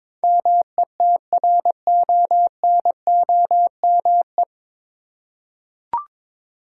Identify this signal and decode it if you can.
Morse code, heard as a keyed tone, about 23 wpm, 700 Hz. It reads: METRONOME
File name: Difficult Words with Hints - 22wpm